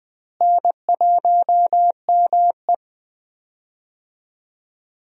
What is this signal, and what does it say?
Morse code, heard as a keyed tone, about 20 wpm, 700 Hz. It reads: N1ME
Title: USA Call Signs - 4 Characters - Repeated 3x - 20wpm